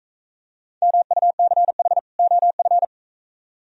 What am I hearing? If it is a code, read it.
Morse code, heard as a keyed tone, about 42 wpm, 700 Hz. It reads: MUCH OF